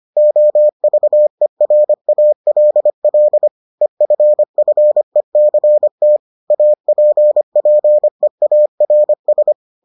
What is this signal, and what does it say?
Morse code, heard as a keyed tone, about 25 wpm, 600 Hz. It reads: OVERALL EFFECT APPEARS